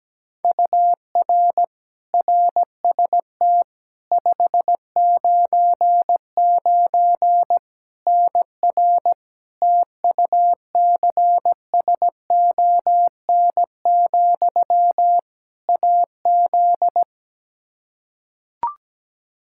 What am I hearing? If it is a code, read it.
Morse code, heard as a keyed tone, about 17 wpm, 700 Hz. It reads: UR RST 599 NR TUCSON, AZ
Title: Practice QSO - Exchange 1 - Beginning - 17wpm